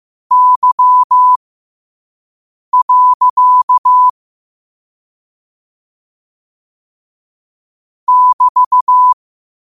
Morse code, heard as a keyed tone, about 15 wpm, spaced as for 5 wpm, 1000 Hz. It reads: Y. <BT>
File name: qotdfiles / 5wpm